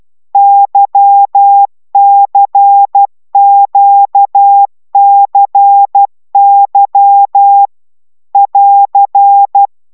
Morse code, heard as a keyed tone, about 12 wpm, 800 Hz. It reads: YCQCY <AR>
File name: taviro tanulas / cw 75 15 10